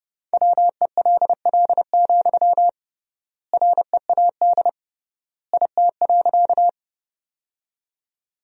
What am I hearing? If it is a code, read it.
Morse code, heard as a keyed tone, about 29 wpm, 700 Hz. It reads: WELL, READ IT.
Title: 3x Word-Spacing - Sentences from Top 200 Words - 30wpm